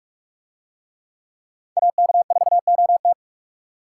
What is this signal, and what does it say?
Morse code, heard as a keyed tone, about 41 wpm, 700 Hz. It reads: AK4OT